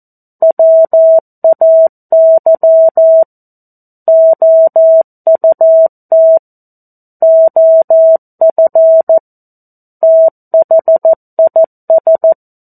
Morse code, heard as a keyed tone, about 14 wpm, 650 Hz. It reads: WAY OUT OF THIS
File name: 02-Chapter-02 14 WPM